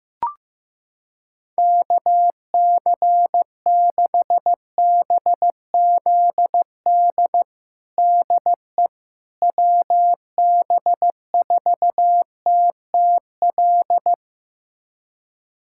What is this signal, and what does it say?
Morse code, heard as a keyed tone, about 15 wpm, 700 Hz. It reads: KC6BZD DE WB4TTL
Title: USA Callsign DE Callsign - 6 Characters - 15wpm